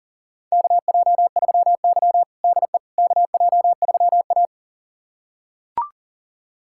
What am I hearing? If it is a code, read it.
Morse code, heard as a keyed tone, about 37 wpm, 700 Hz. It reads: KJ3Y DE KJ3A
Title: USA Callsign DE Callsign - 4 Characters - 40wpm